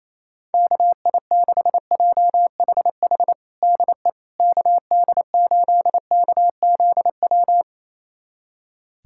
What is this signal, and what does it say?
Morse code, heard as a keyed tone, about 28 wpm, 700 Hz. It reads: KI6JHH DE KD8KZW
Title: USA Callsign DE Callsign 28wpm